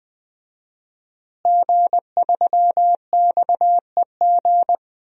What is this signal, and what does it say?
Morse code, heard as a keyed tone, about 20 wpm, 700 Hz. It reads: G3XEG
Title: DXCC-100 Call Signs - 5 Characters - Repeated 3x - 20wpm